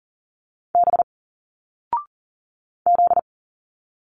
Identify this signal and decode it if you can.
Morse code, heard as a keyed tone, about 40 wpm, 700 Hz. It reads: B Z
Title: Single Letters 40wpm